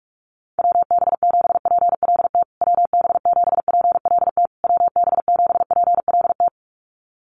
Morse code, heard as a keyed tone, about 44 wpm, 690 Hz. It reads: WB7PLT WB7PLT WB7PLT
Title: USA Call Signs Repeated 3x 45wpm